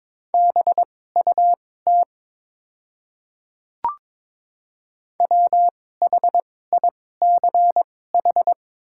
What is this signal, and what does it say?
Morse code, heard as a keyed tone, about 22 wpm, spaced as for 16 wpm, 690 Hz. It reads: BUT WHICH
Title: 2x Character Spacing - Top 300 Words - 22wpm